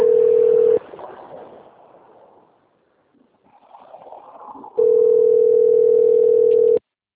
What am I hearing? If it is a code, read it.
Ringback tone (North America), 440 + 480 Hz.
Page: Cosmopolis - SP